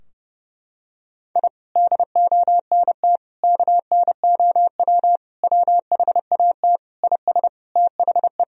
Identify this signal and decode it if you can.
Morse code, heard as a keyed tone, about 30 wpm, 700 Hz. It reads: I DONT KNOW WHAT IS THE
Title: Sentences from Top 1000 Words - Review Part 4 - 30wpm